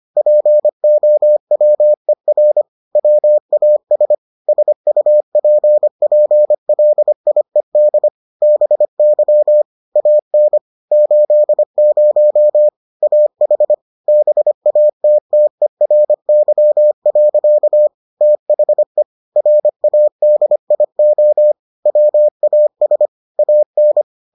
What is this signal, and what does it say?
Morse code, heard as a keyed tone, about 25 wpm, 600 Hz. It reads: POWER WAS SUPPLIED BY AN 80 AH BATTERY. THE RADIO WAS AN